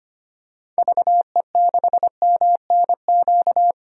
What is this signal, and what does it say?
Morse code, heard as a keyed tone, about 25 wpm, 690 Hz. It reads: VE6MNQ
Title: DXCC-100 Call Signs - 6 Characters - 25wpm